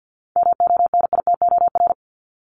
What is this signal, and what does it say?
Morse code, heard as a keyed tone, about 50 wpm, 700 Hz. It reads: MONITOR
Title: Top 1751-2000 Words 50wpm